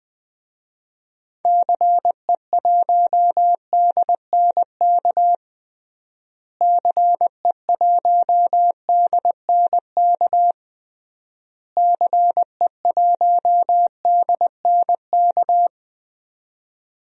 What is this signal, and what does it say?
Morse code, heard as a keyed tone, about 20 wpm, 700 Hz. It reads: CE1DNK CE1DNK CE1DNK
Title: DXCC-100 Call Signs - 6 Characters - Repeated 3x - 20wpm